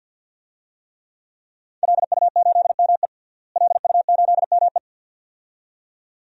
Morse code, heard as a keyed tone, about 51 wpm, 700 Hz. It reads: LU8ME LU8ME
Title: DXCC-100 Call Signs - 5 Characters - Repeated 2x - 50wpm